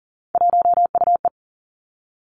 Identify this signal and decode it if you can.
Morse code, heard as a keyed tone, about 38 wpm, 700 Hz. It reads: 1UE
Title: Three Challenging Letter-Number Combinations 40wpm